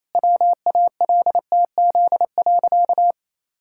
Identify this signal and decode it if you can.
Morse code, heard as a keyed tone, about 28 wpm, 700 Hz. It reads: WALTZ.
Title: Pangram Challenge v2 - 28wpm